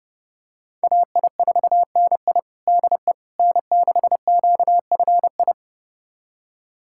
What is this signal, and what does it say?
Morse code, heard as a keyed tone, about 30 wpm, 700 Hz. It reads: AI4NI DE N6QFI
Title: USA Callsign DE Callsign - 5 Characters - 30wpm